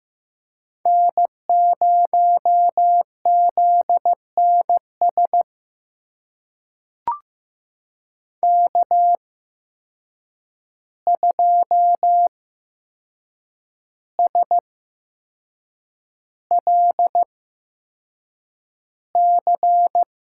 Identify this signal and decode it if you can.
Morse code, heard as a keyed tone, about 15 wpm, spaced as for 6 wpm, 700 Hz. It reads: N0ZNS K2SLC
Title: USA Call Signs - 5 Characters - 8x Character Spacing - 15wpm